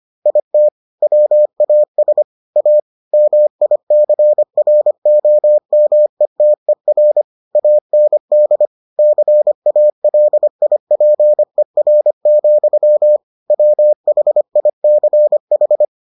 Morse code, heard as a keyed tone, about 25 wpm, 600 Hz. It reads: IT WAS A MICROMETER AND CALIPER, WHICH